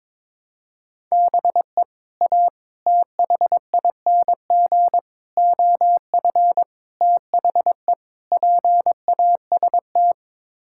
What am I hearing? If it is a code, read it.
Morse code, heard as a keyed tone, about 22 wpm, 700 Hz. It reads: BE A THING OF THE PAST
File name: practice / 22wpm / Sets of 6 Words - 22wpm